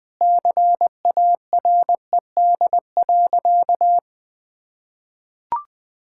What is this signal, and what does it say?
Morse code, heard as a keyed tone, about 20 wpm, 700 Hz. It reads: CARED.
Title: Sentences with 8-Word Phrases - 20wpm